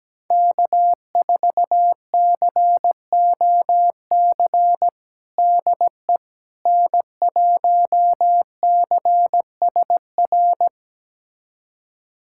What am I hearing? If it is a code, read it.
Morse code, heard as a keyed tone, about 17 wpm, 700 Hz. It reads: K4COC DE N1CSR
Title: USA Callsign DE Callsign 17wpm